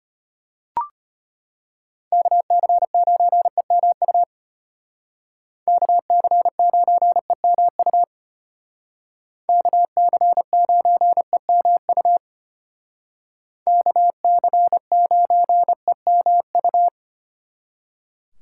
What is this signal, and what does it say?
Morse code, heard as a keyed tone, about 33 wpm, 700 Hz. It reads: KC9EMU KC9EMU KC9EMU KC9EMU